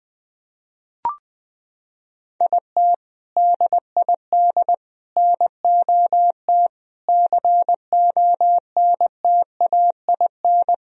Morse code, heard as a keyed tone, about 20 wpm, 700 Hz. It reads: IT DID NOT CONTAIN
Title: Sentences from Top 1000 Words - Review Part 2 - 20wpm